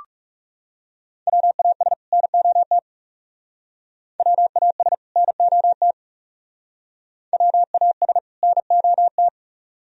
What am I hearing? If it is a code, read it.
Morse code, heard as a keyed tone, about 36 wpm, 700 Hz. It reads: WAS NOT WAS NOT WAS NOT
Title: Speed-Racing - Sets of 2 Words 30wpm